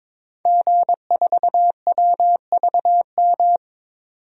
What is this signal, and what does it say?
Morse code, heard as a keyed tone, about 22 wpm, 700 Hz. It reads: G4WVM
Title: DXCC-100 Call Signs - 5 Characters - 22wpm